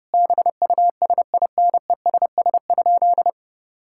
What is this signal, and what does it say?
Morse code, heard as a keyed tone, about 32 wpm, 700 Hz. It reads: BUSINESS?
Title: Sentences from Top 1000 Words - Review Part 6 - 30wpm